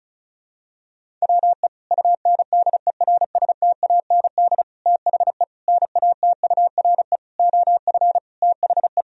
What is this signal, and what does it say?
Morse code, heard as a keyed tone, about 36 wpm, 700 Hz. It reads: WE UNDERSTAND THE NATURE OF THE
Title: Sentences from Top 1000 Words - Review Part 2 - 35wpm